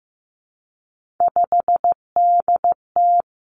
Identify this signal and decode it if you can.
Morse code, heard as a keyed tone, about 15 wpm, 700 Hz. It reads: 5DT
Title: Three Challenging Letter-Number Combinations 15wpm